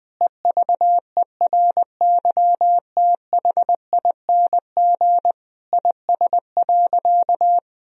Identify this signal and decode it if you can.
Morse code, heard as a keyed tone, about 20 wpm, 700 Hz. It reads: EVERYTHING IS.
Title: Sentences from Top 1000 Words - Review Part 4 - 20wpm